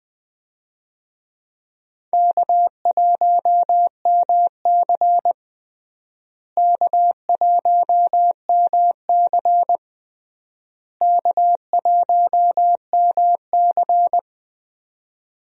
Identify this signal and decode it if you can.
Morse code, heard as a keyed tone, about 20 wpm, 700 Hz. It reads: K1MC K1MC K1MC
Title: USA Call Signs - 4 Characters - Repeated 3x - 20wpm